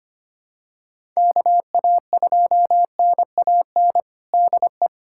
Morse code, heard as a keyed tone, about 25 wpm, 700 Hz. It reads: KA2NAN DE
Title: USA Callsign DE Callsign 25wpm